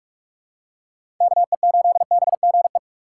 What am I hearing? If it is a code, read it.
Morse code, heard as a keyed tone, about 44 wpm, 700 Hz. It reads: KE8BGE
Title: USA Call Signs - 6 Characters - 6x Character Spacing - 45wpm